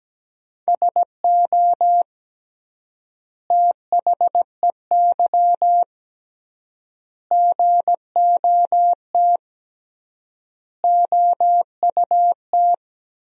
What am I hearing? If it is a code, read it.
Morse code, heard as a keyed tone, about 17 wpm, 700 Hz. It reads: SO THEY GOT OUT
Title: 3x Word-Spacing - Sentences from Top 300 Words - 17wpm